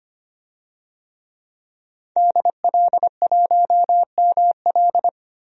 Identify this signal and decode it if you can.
Morse code, heard as a keyed tone, about 25 wpm, 700 Hz. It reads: DL1ML
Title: DXCC-100 Call Signs - 5 Characters - 25wpm